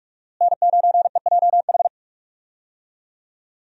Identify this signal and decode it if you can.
Morse code, heard as a keyed tone, about 44 wpm, 700 Hz. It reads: N9EJH
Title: USA Call Signs - 5 Characters - Repeated 3x - 45wpm